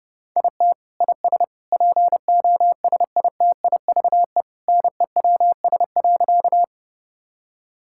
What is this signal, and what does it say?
Morse code, heard as a keyed tone, about 30 wpm, 700 Hz. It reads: IT IS POSITIVE NEWS.
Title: Sentences from Top 1000 Words - Review Part 5 - 30wpm